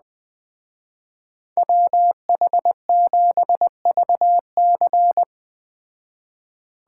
Morse code, heard as a keyed tone, about 21 wpm, 700 Hz. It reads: WH7VC
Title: Speed Racing - USA Call Signs 17wpm